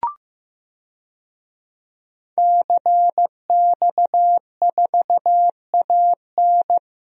Morse code, heard as a keyed tone, about 15 wpm, 700 Hz. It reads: CX4AN
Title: DXCC-100 Call Signs - 5 Characters - Repeated 2x - 15wpm